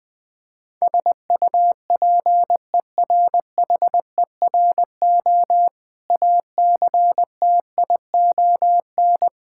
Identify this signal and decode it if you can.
Morse code, heard as a keyed tone, about 20 wpm, 690 Hz. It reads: SUPERHERO ACTION
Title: Difficult Words with Hints - 20wpm